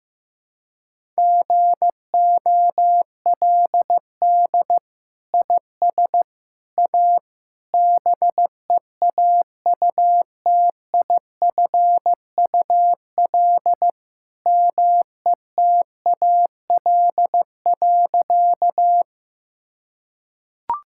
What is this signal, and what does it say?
Morse code, heard as a keyed tone, about 15 wpm, 700 Hz. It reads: GOLD IS A BEAUTIFUL METAL.
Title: Sentences from Top 1000 Words - Review Part 6 - 15wpm